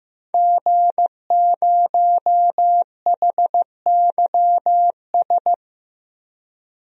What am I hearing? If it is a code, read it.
Morse code, heard as a keyed tone, about 15 wpm, 700 Hz. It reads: G0HYS
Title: DXCC-100 Call Signs - 5 Characters - Repeated 3x - 15wpm